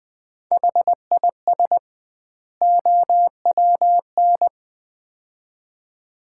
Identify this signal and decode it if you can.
Morse code, heard as a keyed tone, about 20 wpm, 700 Hz. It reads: HIS OWN
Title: 2x Word-Spacing - Sets of 2 Words - 20wpm